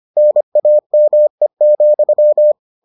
Morse code, heard as a keyed tone, about 25 wpm, 600 Hz. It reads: NAME,